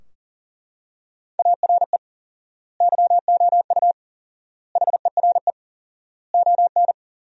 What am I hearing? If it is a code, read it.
Morse code, heard as a keyed tone, about 37 wpm, 700 Hz. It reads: ARE YOU HERE ON
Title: 4x Word-Spacing - Sentences from Top 400 Words - 40wpm